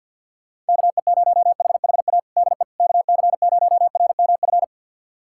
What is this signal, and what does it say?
Morse code, heard as a keyed tone, about 50 wpm, 700 Hz. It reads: KE0HHA DE KC0RMF
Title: USA Callsign DE Callsign 50wpm